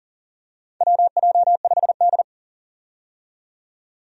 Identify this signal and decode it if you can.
Morse code, heard as a keyed tone, about 37 wpm, 700 Hz. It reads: WJ5D
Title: USA Call Signs - 4 Characters - 40wpm